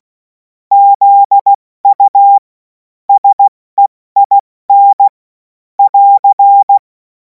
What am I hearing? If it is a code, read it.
Morse code, heard as a keyed tone, about 16 wpm, spaced as for 14 wpm, 800 Hz. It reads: ZU SEIN <AR>
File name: Morsekurs / Witze / Witz 26. Mai 2026 70 F80